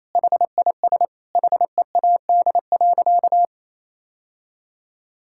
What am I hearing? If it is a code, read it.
Morse code, heard as a keyed tone, about 27 wpm, 700 Hz. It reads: HIS HEAD.
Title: Sentences from Top 1000 Words - Review Part 6 - 28wpm